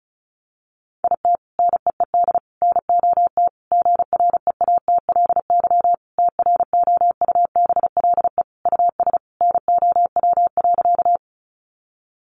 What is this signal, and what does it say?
Morse code, heard as a keyed tone, about 34 wpm, 700 Hz. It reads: IT NEED NOT GREATLY TROUBLE US NOW.
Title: Sentences from Top 1400 Words 35wpm